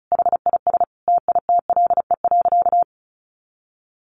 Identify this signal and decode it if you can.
Morse code, heard as a keyed tone, about 34 wpm, 700 Hz. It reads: HIS TITLE.
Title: Sentences from Top 1200 Words 35wpm